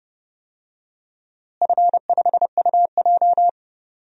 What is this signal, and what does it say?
Morse code, heard as a keyed tone, about 30 wpm, 700 Hz. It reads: F5UJ
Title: DXCC-100 Call Signs - 4 Characters - Repeated 2x - 30wpm